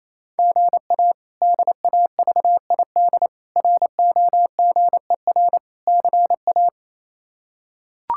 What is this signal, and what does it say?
Morse code, heard as a keyed tone, about 28 wpm, 700 Hz. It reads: GA DAVID ROGER CA
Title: K1USN SST QSO Practice - 28wpm